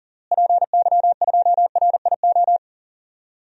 Morse code, heard as a keyed tone, about 40 wpm, 700 Hz. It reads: PY2RIO
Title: DXCC-100 Call Signs - 6 Characters - Repeated 3x - 40wpm